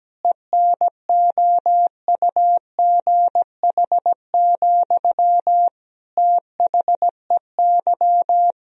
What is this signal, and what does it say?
Morse code, heard as a keyed tone, about 17 wpm, 700 Hz. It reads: ENOUGH, THEY
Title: Sentences from Top 1000 Words - Review Part 2 - 17wpm